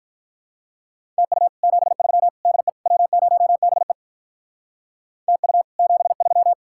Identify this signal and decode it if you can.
Morse code, heard as a keyed tone, about 42 wpm, 700 Hz. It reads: TU 73 DE W0BE TU 73
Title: Speed-Racing Common QSO Phrases Encore - 35wpm